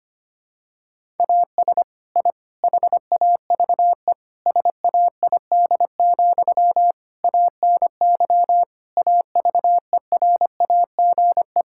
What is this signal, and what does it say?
Morse code, heard as a keyed tone, about 25 wpm, 700 Hz. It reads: AS I HAVE SAID, ANY AVERAGE
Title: Sentences from Top 1000 Words - Review Part 6 - 25wpm